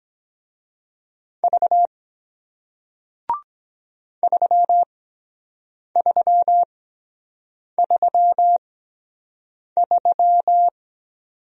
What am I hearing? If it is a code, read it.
Morse code, heard as a keyed tone, about 23 wpm, spaced as for 7 wpm, 700 Hz. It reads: V 3333